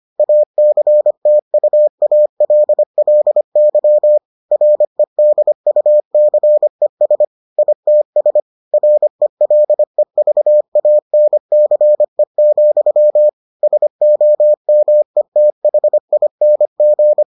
Morse code, heard as a keyed tone, about 25 wpm, 600 Hz. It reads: ACTUALLY REDUCES ITS RELEVANCE, SOMETHING